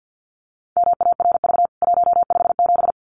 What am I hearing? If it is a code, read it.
Morse code, heard as a keyed tone, about 49 wpm, 700 Hz. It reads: MARV 157
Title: CWT Exchange 50wpm